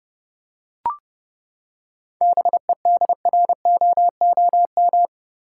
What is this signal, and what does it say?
Morse code, heard as a keyed tone, about 30 wpm, 700 Hz. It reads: BEDROOM
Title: Seven Letter Words - 30wpm